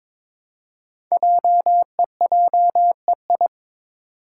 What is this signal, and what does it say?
Morse code, heard as a keyed tone, about 22 wpm, 690 Hz. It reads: JEJEI
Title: Code Groups - 5 Letter:Numbers - 22wpm